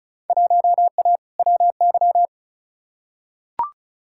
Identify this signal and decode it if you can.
Morse code, heard as a keyed tone, about 34 wpm, 700 Hz. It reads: 1A WY
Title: Field Day Exchanges - 35wpm